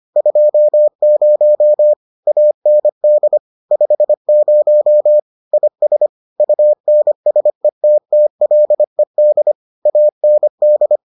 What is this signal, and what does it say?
Morse code, heard as a keyed tone, about 25 wpm, 600 Hz. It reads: 20 AND 50 IS UNSETTLED AND